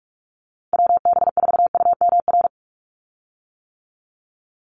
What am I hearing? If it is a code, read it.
Morse code, heard as a keyed tone, about 44 wpm, 700 Hz. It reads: WB4UMR
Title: USA Call Signs 45wpm